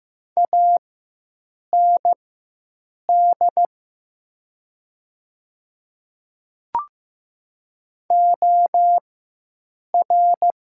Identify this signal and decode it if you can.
Morse code, heard as a keyed tone, about 15 wpm, spaced as for 7 wpm, 700 Hz. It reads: AND OR
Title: 4x Character Spacing - Top 100 Words - 15wpm